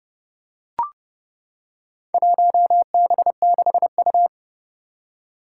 Morse code, heard as a keyed tone, about 30 wpm, 700 Hz. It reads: 1B6U
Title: Four Challenging Letter-Number Combinations 30wpm